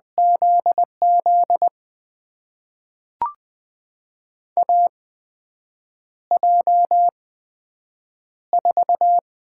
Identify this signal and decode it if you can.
Morse code, heard as a keyed tone, about 20 wpm, spaced as for 7 wpm, 700 Hz. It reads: ZZ AJ4
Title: USA Call Signs - 8x Character Spacing - 20wpm